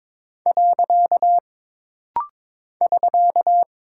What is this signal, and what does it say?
Morse code, heard as a keyed tone, about 22 wpm, 700 Hz. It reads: . <SK>
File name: practice / 22wpm / Single Punctuation-Prosign - Rapid-Fire - 22wpm